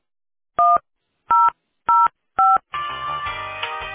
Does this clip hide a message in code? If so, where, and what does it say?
Yes: DTMF from 0.6 to 2.6 s, heard as telephone touch tones, tones of about 200 ms. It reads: 1005